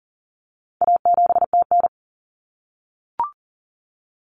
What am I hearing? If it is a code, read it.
Morse code, heard as a keyed tone, about 40 wpm, 700 Hz. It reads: A7TN
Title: Common Cut Numbers - 4-Digit 40wpm